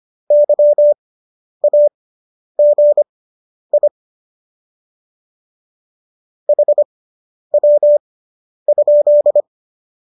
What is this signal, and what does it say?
Morse code, heard as a keyed tone, about 25 wpm, spaced as for 10 wpm, 590 Hz. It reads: YAGI HW?